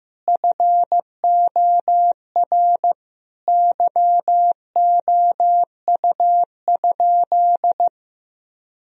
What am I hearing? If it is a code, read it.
Morse code, heard as a keyed tone, about 15 wpm, 700 Hz. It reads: FOR YOU?
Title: Sentences from Top 1000 Words - Review Part 2 - 15wpm